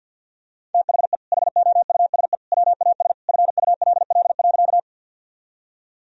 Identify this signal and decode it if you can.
Morse code, heard as a keyed tone, about 52 wpm, 700 Hz. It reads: THE HOUSE WAS FULL.
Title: Sentences from Top 1000 Words - Review Part 3 - 50wpm